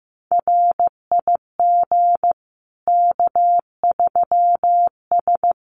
Morse code, heard as a keyed tone, about 15 wpm, 700 Hz. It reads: RIG K3S
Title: Common QSO Phrases 15wpm